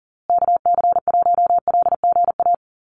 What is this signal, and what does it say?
Morse code, heard as a keyed tone, about 40 wpm, 700 Hz. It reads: KC1LGA
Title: USA Call Signs Repeated 3x 40wpm